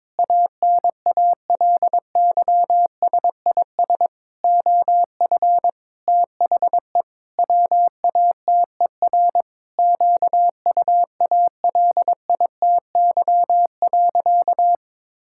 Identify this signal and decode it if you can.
Morse code, heard as a keyed tone, about 22 wpm, 700 Hz. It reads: ANALYSIS OF THE WATER QUALITY.
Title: Sentences from Top 1000 Words - Review Part 5 - 22wpm